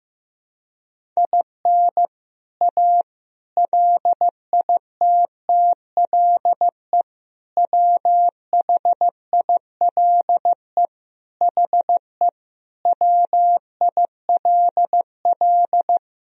Morse code, heard as a keyed tone, about 15 wpm, 700 Hz. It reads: IN A LITTLE WHILE HE WILL
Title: Sentences from Top 1000 Words - Review Part 1 - 15wpm